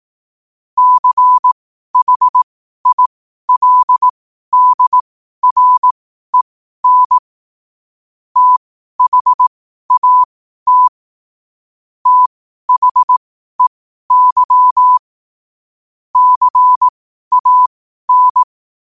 Morse code, heard as a keyed tone, about 18 wpm, spaced as for 12 wpm, 1000 Hz. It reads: CHILDREN THAT THEY CAN
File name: qotdfiles / 12wpm